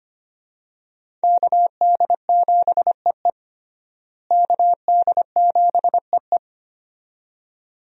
Morse code, heard as a keyed tone, about 25 wpm, 700 Hz. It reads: KD7EE KD7EE
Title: USA Call Signs - 5 Characters - Repeated 2x - 25wpm